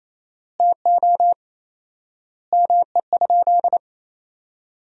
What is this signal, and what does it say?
Morse code, heard as a keyed tone, about 28 wpm, 700 Hz. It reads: TO ME?
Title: 4x Word-Spacing - Sentences from Top 200 Words - 28wpm